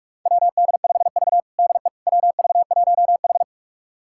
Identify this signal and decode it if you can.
Morse code, heard as a keyed tone, about 42 wpm, 690 Hz. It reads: WD5V DE WV1H
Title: USA Callsign DE Callsign - 4 Characters - 45wpm